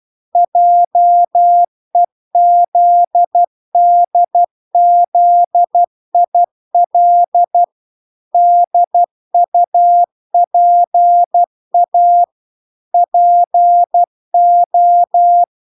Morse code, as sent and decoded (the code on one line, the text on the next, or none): .--- . --.. -.. --.. .. .-.. / -.. ..- .--. .- / .--. ---
JEZDZIL DUPA PO